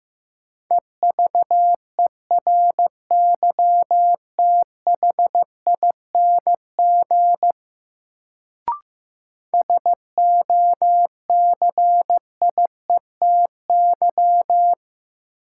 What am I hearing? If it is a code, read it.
Morse code, heard as a keyed tone, about 16 wpm, 700 Hz. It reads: EVERYTHING SOCIETY
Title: Sentences from Top 700 Words - New Word Drill - 15wpm